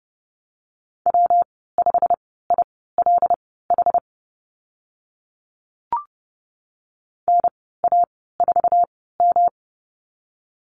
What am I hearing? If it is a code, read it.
Morse code, heard as a keyed tone, about 30 wpm, spaced as for 17 wpm, 700 Hz. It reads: W5ILH NA4M
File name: Lesson 19g - Letter w - Call Signs 30wpm